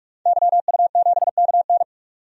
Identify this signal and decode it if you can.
Morse code, heard as a keyed tone, about 46 wpm, 690 Hz. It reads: YU7KN
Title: DXCC-100 Call Signs - 5 Characters - 45wpm